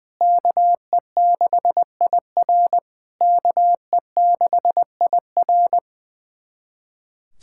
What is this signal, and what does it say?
Morse code, heard as a keyed tone, about 21 wpm, 700 Hz. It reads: KE6IR KE6IR
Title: SOTA QSO Practice 20wpm